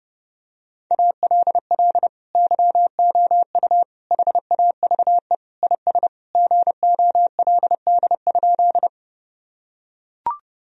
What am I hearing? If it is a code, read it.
Morse code, heard as a keyed tone, about 32 wpm, 700 Hz. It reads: ALL YOU HAVE IS GOLD?
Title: Sentences from Top 1000 Words - Review Part 3 - 30wpm